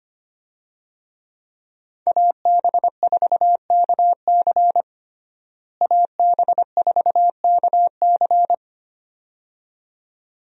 Morse code, heard as a keyed tone, about 25 wpm, 700 Hz. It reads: AB4KC AB4KC
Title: USA Call Signs - 5 Characters - Repeated 2x - 25wpm